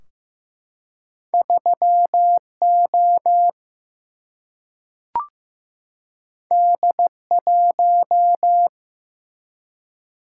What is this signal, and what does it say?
Morse code, heard as a keyed tone, about 15 wpm, 700 Hz. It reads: 3 O D 1
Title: Single-Letter Single-Number 15wpm